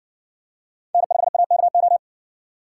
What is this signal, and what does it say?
Morse code, heard as a keyed tone, about 58 wpm, 700 Hz. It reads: N5AKO